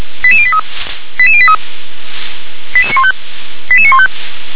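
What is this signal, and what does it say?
Five-tone selective call (ZVEI), heard as quick runs of five tones, tones of about 70 ms. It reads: 88083 88083 88915 88915
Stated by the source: ZVEI five-tone selective call 88783 received by SDR.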